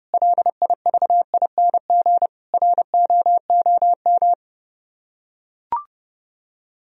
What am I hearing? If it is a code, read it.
Morse code, heard as a keyed tone, about 31 wpm, 700 Hz. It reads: LIVING ROOM